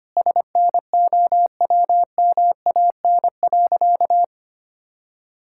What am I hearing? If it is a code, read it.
Morse code, heard as a keyed tone, about 25 wpm, 700 Hz. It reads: SNOWMAN.
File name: practice / 25wpm / All-Age Appropriate Jokes - 25wpm